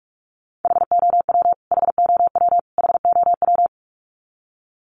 Morse code, heard as a keyed tone, about 44 wpm, 700 Hz. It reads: HOW HOW HOW
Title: Top 500 Words Repeated 45wpm